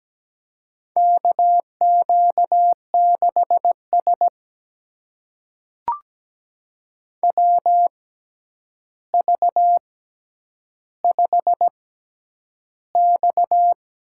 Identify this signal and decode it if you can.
Morse code, heard as a keyed tone, about 17 wpm, spaced as for 9 wpm, 700 Hz. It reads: KQ6S WV5X
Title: USA Call Signs - 4 Characters - 6x Character Spacing - 17wpm